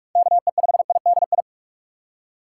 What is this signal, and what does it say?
Morse code, heard as a keyed tone, about 47 wpm, 700 Hz. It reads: KE5IDI
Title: USA Call Signs - 2x Character Spacing - 45wpm